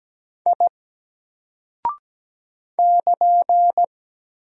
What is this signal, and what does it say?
Morse code, heard as a keyed tone, about 17 wpm, 700 Hz. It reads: I <KN>
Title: Single Character (Letter:Number:Punctuation:Prosign) 17wpm